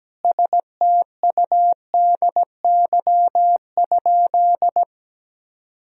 Morse code, heard as a keyed tone, about 17 wpm, 700 Hz. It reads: STUDY?
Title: Sentences from Top 1000 Words - Review Part 2 - 17wpm